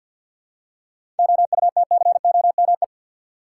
Morse code, heard as a keyed tone, about 52 wpm, 700 Hz. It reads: OUTCOME
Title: Seven Letter Words - 50wpm